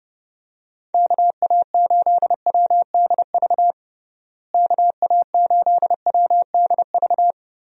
Morse code, heard as a keyed tone, about 32 wpm, 700 Hz. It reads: KA8WDV KA8WDV
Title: USA Call Signs - 6 Characters - Repeated 3x - 30wpm